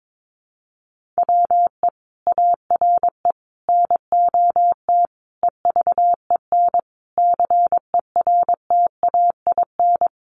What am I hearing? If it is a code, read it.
Morse code, heard as a keyed tone, about 22 wpm, 700 Hz. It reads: WE ARE NOT EVEN CERTAIN
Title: Sentences from Top 1500 Words 22wpm